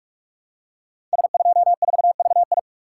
Morse code, heard as a keyed tone, about 44 wpm, 700 Hz. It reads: S24VI
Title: Code Groups - 5 Letter:Numbers - 45wpm